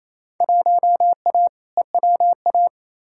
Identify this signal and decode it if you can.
Morse code, heard as a keyed tone, about 28 wpm, 700 Hz. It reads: 1A EWA
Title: Field Day Exchanges Encore - 28wpm